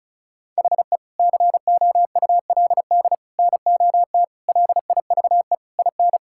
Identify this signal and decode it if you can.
Morse code, heard as a keyed tone, about 36 wpm, 700 Hz. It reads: HE COULD NOT LIVE IN
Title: Sentences from Top 1600 Words 35wpm